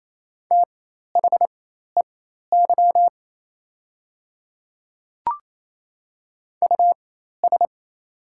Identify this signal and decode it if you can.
Morse code, heard as a keyed tone, about 28 wpm, spaced as for 13 wpm, 700 Hz. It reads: THEY US
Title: 4x Character Spacing - Top 500 Words - 28wpm